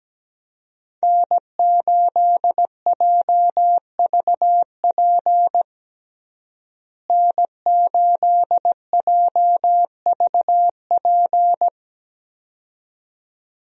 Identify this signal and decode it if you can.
Morse code, heard as a keyed tone, about 17 wpm, 700 Hz. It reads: N8JVP N8JVP
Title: USA Call Signs - 5 Characters - Repeated 2x - 17wpm